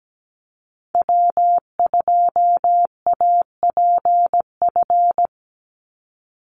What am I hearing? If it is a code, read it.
Morse code, heard as a keyed tone, about 17 wpm, 700 Hz. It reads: W2APF
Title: Call Signs Repeated 2x - 17wpm